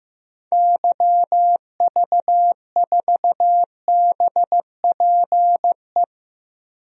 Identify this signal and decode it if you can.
Morse code, heard as a keyed tone, about 15 wpm, 700 Hz. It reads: YV4BPE
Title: DXCC-100 Call Signs - 6 Characters - 15wpm